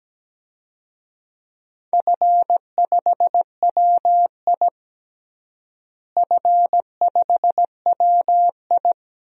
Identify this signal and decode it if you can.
Morse code, heard as a keyed tone, about 17 wpm, 700 Hz. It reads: F5WI F5WI
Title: DXCC-100 Call Signs - 4 Characters - Repeated 3x - 17wpm